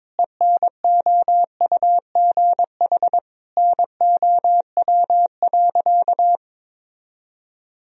Morse code, heard as a keyed tone, about 22 wpm, 700 Hz. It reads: ENOUGH NOW.
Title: Sentences from Top 1000 Words - Review Part 2 - 22wpm